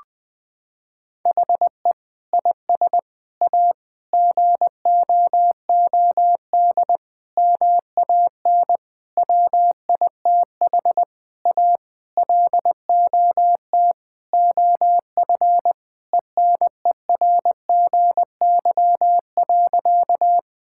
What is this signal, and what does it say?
Morse code, heard as a keyed tone, about 20 wpm, 700 Hz. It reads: HE IS A GOOD MAN WITH A LOT OF ENERGY.
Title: Sentences from Top 1000 Words - Review Part 6 - 20wpm